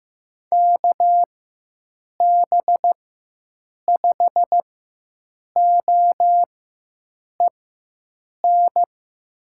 Morse code, heard as a keyed tone, about 15 wpm, spaced as for 7 wpm, 700 Hz. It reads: KB5OEN
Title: USA Call Signs - 6 Characters - 4x Character Spacing - 15wpm